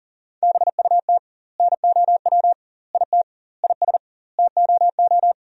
Morse code, heard as a keyed tone, about 40 wpm, 700 Hz. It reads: BUT NOW IT IS TOO
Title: 2x Word-Spacing - Sentences from Top 300 Words - 40wpm